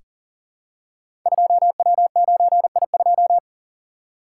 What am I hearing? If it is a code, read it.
Morse code, heard as a keyed tone, about 41 wpm, 700 Hz. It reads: 2W9I2
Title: Code Groups - 5 Letter:Numbers - 40wpm